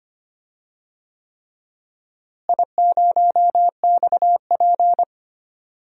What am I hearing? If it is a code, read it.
Morse code, heard as a keyed tone, about 25 wpm, 700 Hz. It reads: I0XP
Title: DXCC-100 Call Signs - 4 Characters - Repeated 3x - 25wpm